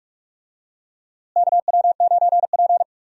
Morse code, heard as a keyed tone, about 44 wpm, 700 Hz. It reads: KW9P
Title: USA Call Signs - 4 Characters - Repeated 3x - 45wpm